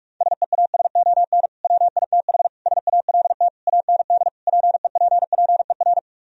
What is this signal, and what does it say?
Morse code, heard as a keyed tone, about 44 wpm, 700 Hz. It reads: SEASON WITH SALT AND PEPPER